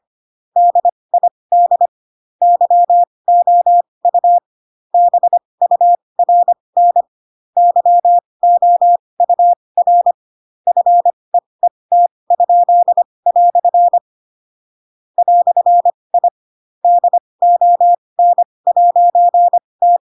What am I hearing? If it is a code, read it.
Morse code, heard as a keyed tone, about 25 wpm, spaced as for 20 wpm, 700 Hz. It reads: DID YOU BURN YOUR FEET?" "I DON'T
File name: Pinocchio / Pinocchio ch7 15 25 ik1hge